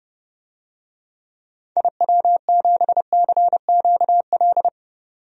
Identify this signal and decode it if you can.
Morse code, heard as a keyed tone, about 32 wpm, 700 Hz. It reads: IW7CQL
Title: DXCC-100 Call Signs - 6 Characters - 30wpm